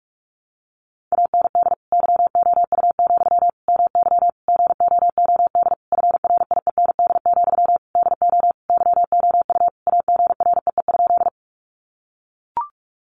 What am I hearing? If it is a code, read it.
Morse code, heard as a keyed tone, about 44 wpm, 700 Hz. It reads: AND YOU, MY GOOD FRIEND, DO YOU AGREE?
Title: Sentences from Top 1100 Words 45wpm